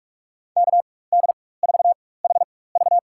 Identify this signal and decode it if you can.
Morse code, heard as a keyed tone, about 42 wpm, spaced as for 20 wpm, 690 Hz. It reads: KD4HV
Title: USA Call Signs - 5 Characters - 4x Character Spacing - 45wpm